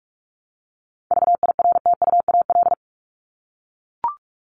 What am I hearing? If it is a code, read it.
Morse code, heard as a keyed tone, about 44 wpm, 700 Hz. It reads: VIRTUAL
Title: LY Suffix - Top 100 Words with Base Words 45wpm